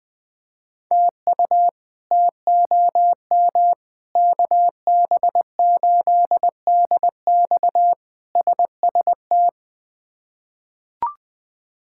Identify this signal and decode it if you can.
Morse code, heard as a keyed tone, about 20 wpm, 700 Hz. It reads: TU TOM KB8DX SST
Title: K1USN SST QSO Practice - 20wpm